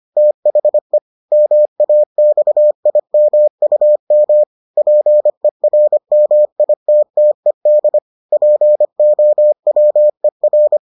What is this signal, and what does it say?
Morse code, heard as a keyed tone, about 25 wpm, 600 Hz. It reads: THE MAXIMUM PERMITTED POWER